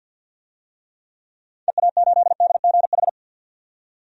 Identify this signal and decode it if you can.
Morse code, heard as a keyed tone, about 52 wpm, 700 Hz. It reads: EA8DGH
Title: DXCC-100 Call Signs - 6 Characters - 50wpm